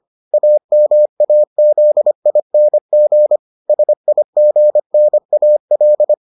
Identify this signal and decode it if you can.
Morse code, heard as a keyed tone, about 25 wpm, 600 Hz. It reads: AMAZING SIGNAL